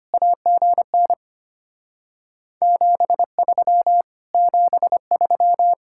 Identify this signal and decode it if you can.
Morse code, heard as a keyed tone, about 24 wpm, 700 Hz. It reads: AGN 73 73
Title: Speed-Racing Common QSO Phrases - 25wpm